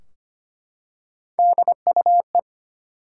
Morse code, heard as a keyed tone, about 25 wpm, 700 Hz. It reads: DUE
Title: Three Letter Words - 25wpm